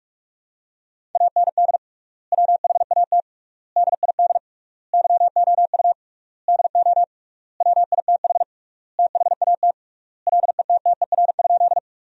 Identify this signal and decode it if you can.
Morse code, heard as a keyed tone, about 42 wpm, 700 Hz. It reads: AND WHAT DID YOU DO WITH THAT LETTER?
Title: 3x Word-Spacing - Sentences from Top 300 Words - 45wpm